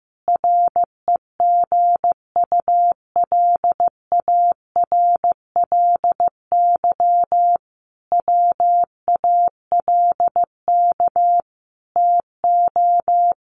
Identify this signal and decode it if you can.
Morse code, heard as a keyed tone, about 14 wpm, 700 Hz. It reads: REGULARLY WALK TO
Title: Sentences from Top 1400 Words 15wpm